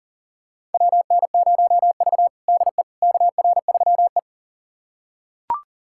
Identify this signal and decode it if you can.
Morse code, heard as a keyed tone, about 40 wpm, 700 Hz. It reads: WN0V DE KR3E
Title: USA Callsign DE Callsign - 4 Characters - 40wpm